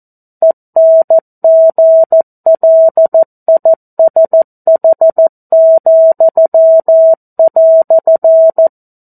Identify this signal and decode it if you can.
Morse code, heard as a keyed tone, about 14 wpm, 650 Hz. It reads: ENGLISH,"
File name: 02-Chapter-02 14 WPM